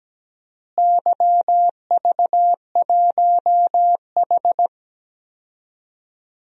Morse code, heard as a keyed tone, about 17 wpm, 700 Hz. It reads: YV1H
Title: DXCC-100 Call Signs - 4 Characters - Repeated 2x - 17wpm